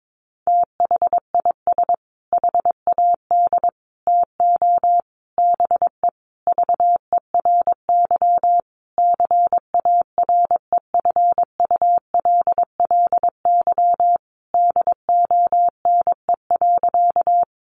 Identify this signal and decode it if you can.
Morse code, heard as a keyed tone, about 22 wpm, 700 Hz. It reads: THIS HAD TO BE VERY CAREFULLY DONE.
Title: Sentences from Top 1300 Words 22wpm